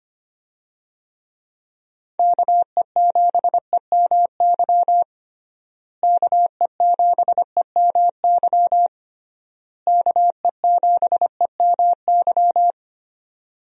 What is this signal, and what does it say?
Morse code, heard as a keyed tone, about 25 wpm, 700 Hz. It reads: KE7EMY KE7EMY KE7EMY
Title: USA Call Signs - 6 Characters - Repeated 3x - 25wpm